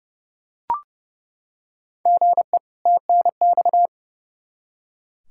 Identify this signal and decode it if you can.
Morse code, heard as a keyed tone, about 30 wpm, 700 Hz. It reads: GE TNX
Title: Practice QSO - Exchange 1 - Beginning - 30wpm